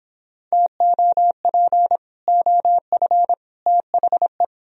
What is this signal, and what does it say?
Morse code, heard as a keyed tone, about 31 wpm, 700 Hz. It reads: TOP OF THE